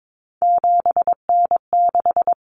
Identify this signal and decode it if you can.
Morse code, heard as a keyed tone, about 22 wpm, 700 Hz. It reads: 7N6
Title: Common Cut Numbers - 3-Digit 22wpm